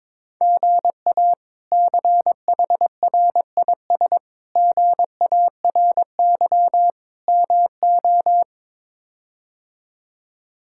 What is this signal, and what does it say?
Morse code, heard as a keyed tone, about 21 wpm, 700 Hz. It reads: GA CHRIS GARY MO
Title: K1USN SST QSO Practice - 22wpm